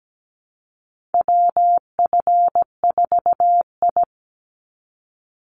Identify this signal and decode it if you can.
Morse code, heard as a keyed tone, about 17 wpm, 700 Hz. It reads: WF4I
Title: NA SOTA Chaser Callsigns 17wpm